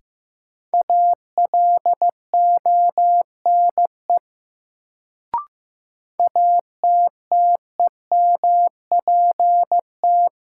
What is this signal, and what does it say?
Morse code, heard as a keyed tone, about 15 wpm, 700 Hz. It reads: ALONE ATTEMPT
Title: Sentences from Top 1000 Words - New Word Drill - 15wpm